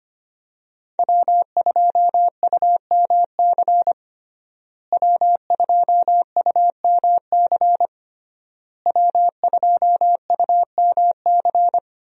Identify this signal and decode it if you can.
Morse code, heard as a keyed tone, about 25 wpm, 700 Hz. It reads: W2UMC W2UMC W2UMC
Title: USA Call Signs - 5 Characters - Repeated 3x - 25wpm